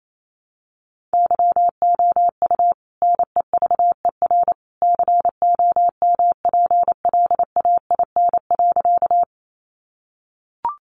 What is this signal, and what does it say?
Morse code, heard as a keyed tone, about 28 wpm, 700 Hz. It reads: YOU NEVER COMPLAIN.
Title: Sentences from Top 1300 Words 28wpm